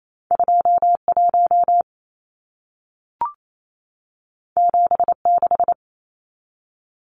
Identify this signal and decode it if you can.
Morse code, heard as a keyed tone, about 28 wpm, 700 Hz. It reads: 21 76
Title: Numbers - 2-Digit 28wpm